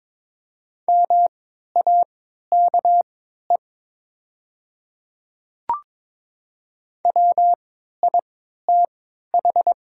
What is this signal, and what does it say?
Morse code, heard as a keyed tone, about 22 wpm, spaced as for 13 wpm, 700 Hz. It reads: MAKE WITH
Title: 3x Character Spacing - Top 100 Words - 22wpm